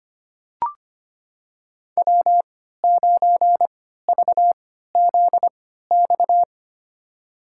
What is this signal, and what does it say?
Morse code, heard as a keyed tone, about 25 wpm, spaced as for 14 wpm, 700 Hz. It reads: W9VZX
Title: USA Call Signs - 5 Characters - 3x Character Spacing - 25wpm